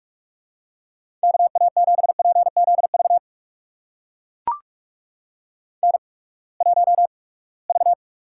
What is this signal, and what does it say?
Morse code, heard as a keyed tone, about 44 wpm, spaced as for 25 wpm, 700 Hz. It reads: KA7PZV N1V
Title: USA Call Signs - 8x Character Spacing - 45wpm